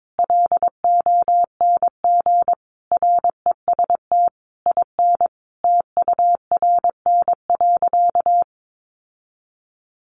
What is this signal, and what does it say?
Morse code, heard as a keyed tone, about 22 wpm, 700 Hz. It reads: LONG REST IN TURN.
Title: Sentences from Top 700 Words 22wpm